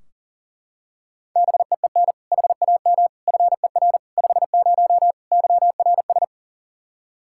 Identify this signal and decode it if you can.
Morse code, heard as a keyed tone, about 40 wpm, 700 Hz. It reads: BEEN HAM FER 50 YRS
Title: Practice QSO - Exchange 3 - More Info - 40wpm